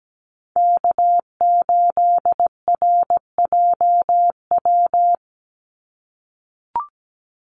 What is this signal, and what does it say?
Morse code, heard as a keyed tone, about 17 wpm, 700 Hz. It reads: K8RJW
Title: Call Signs - 17wpm